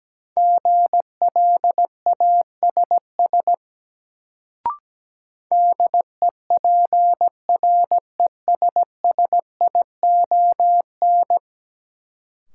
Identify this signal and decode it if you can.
Morse code, heard as a keyed tone, about 17 wpm, 700 Hz. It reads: GLASS DEPRESSION
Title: Sentences from Top 1300 Words - New Word Drill Alt - 17wpm